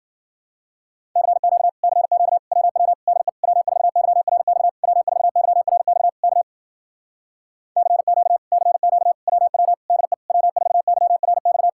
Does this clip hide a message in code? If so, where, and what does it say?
Yes: Morse code from 1.2 to 11.7 s, heard as a keyed tone, about 53 wpm, 700 Hz. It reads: CQ CQ WW DE W4YRX W4YRX K CQ CQ WW DE W4YRX